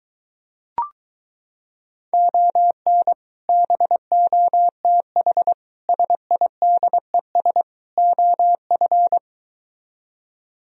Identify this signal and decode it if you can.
Morse code, heard as a keyed tone, about 23 wpm, 700 Hz. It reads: ON BOTH SIDES OF